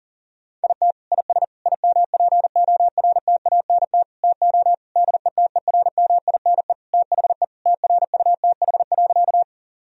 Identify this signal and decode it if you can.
Morse code, heard as a keyed tone, about 37 wpm, 700 Hz. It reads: IT IS IMPORTANT TO DETERMINE THE TRUTH.
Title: Sentences from Top 1000 Words - Review Part 4 - 40wpm